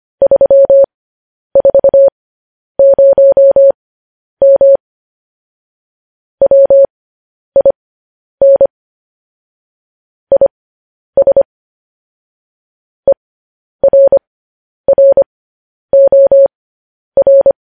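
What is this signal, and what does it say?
Morse code, heard as a keyed tone, about 25 wpm, spaced as for 10 wpm, 550 Hz. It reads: 340M WIN IS ERROR
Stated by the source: Morse code news headlines practice transmission.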